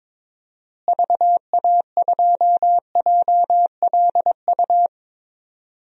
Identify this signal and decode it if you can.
Morse code, heard as a keyed tone, about 22 wpm, 700 Hz. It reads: VA2JLU
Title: DXCC-100 Call Signs - 6 Characters - Repeated 3x - 22wpm